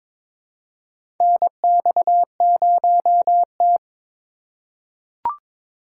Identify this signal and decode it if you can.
Morse code, heard as a keyed tone, about 21 wpm, 700 Hz. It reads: NX0T
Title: USA Call Signs - 4 Characters - 2x Character Spacing - 22wpm